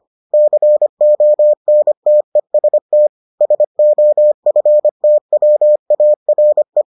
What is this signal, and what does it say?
Morse code, heard as a keyed tone, about 25 wpm, 600 Hz. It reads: CONTEST SOFTWARE